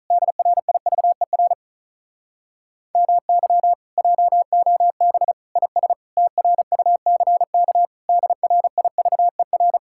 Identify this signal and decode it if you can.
Morse code, heard as a keyed tone, about 44 wpm, 700 Hz. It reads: DRIVER MY JOB IS TRUCK DRIVER